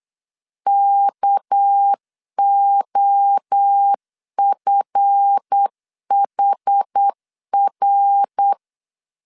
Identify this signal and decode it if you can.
Morse code, heard as a keyed tone, about 8 wpm, 800 Hz. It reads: KOFHR
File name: LEZIONI / 8 YJ45 / 8 YJ4 8wpm 3246